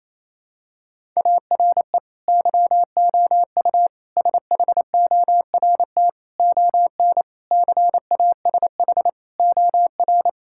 Morse code, heard as a keyed tone, about 28 wpm, 700 Hz. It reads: ARE YOU SHORT ON CASH OR
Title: Sentences from Top 1000 Words - Review Part 5 - 28wpm